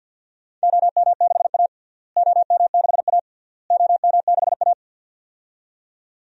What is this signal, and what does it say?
Morse code, heard as a keyed tone, about 50 wpm, 700 Hz. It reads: OM6A OM6A OM6A
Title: DXCC-100 Call Signs - 4 Characters - Repeated 3x - 50wpm